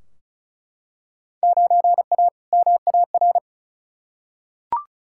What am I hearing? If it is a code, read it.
Morse code, heard as a keyed tone, about 36 wpm, 700 Hz. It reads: 9A MAR
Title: Field Day Exchanges Encore - 35wpm